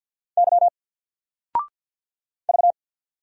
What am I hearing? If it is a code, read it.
Morse code, heard as a keyed tone, about 49 wpm, 700 Hz. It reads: Y V
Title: Single Letter-Number 50wpm